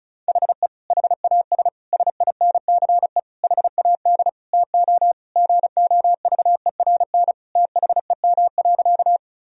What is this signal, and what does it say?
Morse code, heard as a keyed tone, about 36 wpm, 700 Hz. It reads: HE HAS SINCE HAD TO GOVERN THEM.
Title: Sentences from Top 1000 Words - Review Part 2 - 35wpm